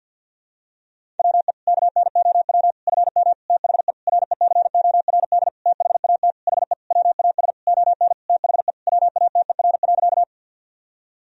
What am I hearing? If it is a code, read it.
Morse code, heard as a keyed tone, about 52 wpm, 700 Hz. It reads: WE KNOW FM THE RECORD THAT HE WAS ON THE WATER.
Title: Sentences from Top 500 Words and Abbreviations 50wpm